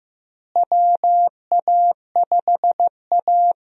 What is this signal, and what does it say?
Morse code, heard as a keyed tone, about 15 wpm, 700 Hz. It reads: WA5A
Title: USA Call Signs - 4 Characters - Repeated 2x - 15wpm